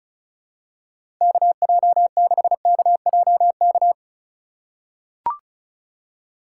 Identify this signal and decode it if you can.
Morse code, heard as a keyed tone, about 34 wpm, 700 Hz. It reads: KJ6KJK
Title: USA Call Signs - 6 Characters - 2x Character Spacing - 35wpm